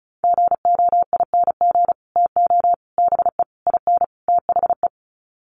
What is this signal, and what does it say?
Morse code, heard as a keyed tone, about 34 wpm, 700 Hz. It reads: GOING TO BE IN THE
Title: Sets of 5 Words Encore 35wpm